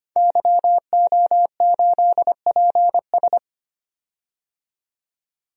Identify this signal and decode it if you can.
Morse code, heard as a keyed tone, about 25 wpm, 700 Hz. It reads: YO8PS
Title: DXCC-100 Call Signs - 5 Characters - 25wpm